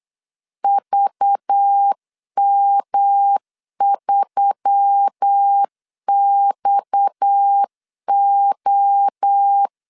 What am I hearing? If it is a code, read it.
Morse code, heard as a keyed tone, about 8 wpm, 800 Hz. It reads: VM3XO
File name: LEZIONI / 9 286 8wpm 3425